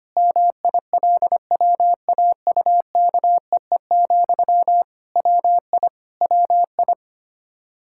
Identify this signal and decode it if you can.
Morse code, heard as a keyed tone, about 25 wpm, 700 Hz. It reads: MILWAUKEE, WI WI
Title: Practice QSO - Exchange 1 - Beginning - 25wpm